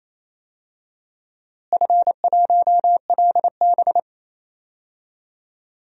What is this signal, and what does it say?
Morse code, heard as a keyed tone, about 28 wpm, 700 Hz. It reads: F1LB
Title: DXCC-100 Call Signs - 4 Characters - Repeated 2x - 28wpm